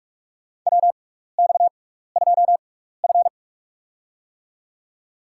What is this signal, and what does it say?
Morse code, heard as a keyed tone, about 44 wpm, spaced as for 15 wpm, 700 Hz. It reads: WX2F
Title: USA Call Signs - 4 Characters - 6x Character Spacing - 45wpm